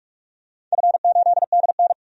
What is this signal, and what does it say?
Morse code, heard as a keyed tone, about 44 wpm, 700 Hz. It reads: F8DN
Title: DXCC-100 Call Signs - 4 Characters - 45wpm